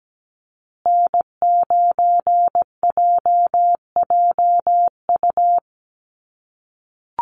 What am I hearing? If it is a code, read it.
Morse code, heard as a keyed tone, about 17 wpm, 700 Hz. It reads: N9JJU
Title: USA Call Signs Repeated 3x 17wpm